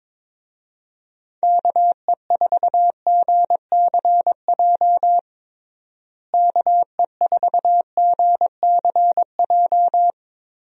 Morse code, heard as a keyed tone, about 22 wpm, 700 Hz. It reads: KE4GCJ KE4GCJ
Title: USA Call Signs - 6 Characters - Repeated 3x - 22wpm